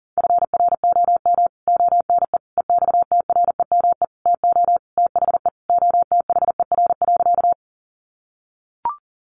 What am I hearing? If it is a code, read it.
Morse code, heard as a keyed tone, about 40 wpm, 700 Hz. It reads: FROM ONE EXTREME TO THE OTHER.
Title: Sentences from Top 1300 Words 40wpm